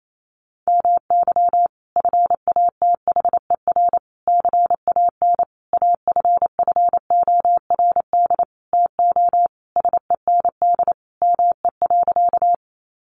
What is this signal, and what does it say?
Morse code, heard as a keyed tone, about 28 wpm, 700 Hz. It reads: MY FATHER CAN AFFORD TO SEND ME.
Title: Sentences from Top 1500 Words 28wpm